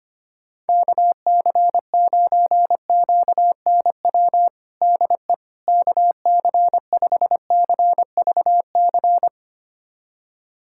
Morse code, heard as a keyed tone, about 25 wpm, 700 Hz. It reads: KC9QNW DE KC5CVC
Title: USA Callsign DE Callsign 25wpm